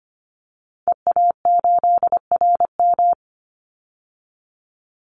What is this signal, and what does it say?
Morse code, heard as a keyed tone, about 25 wpm, 700 Hz. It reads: EA8RM
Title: Call Signs Repeated 2x - 25wpm